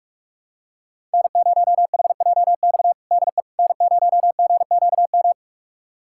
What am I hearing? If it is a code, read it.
Morse code, heard as a keyed tone, about 46 wpm, 700 Hz. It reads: N0HJX DE N0GQM
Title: USA Callsign DE Callsign - 5 Characters - 45wpm